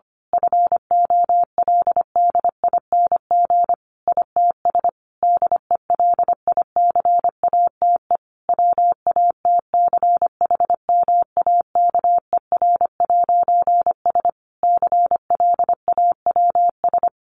Morse code, heard as a keyed tone, about 25 wpm, 700 Hz. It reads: FOLDING ITS DELICATE WATCHMAKER'S CLAWS
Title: A Sound Of Thunder 25wpm